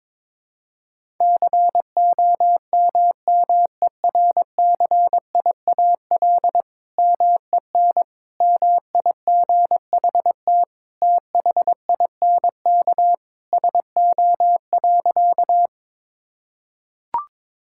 Morse code, heard as a keyed tone, about 21 wpm, 700 Hz. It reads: COMMERCIAL MEN MIGHT THINK SO.
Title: Sentences from Top 1000 Words - Review Part 5 - 22wpm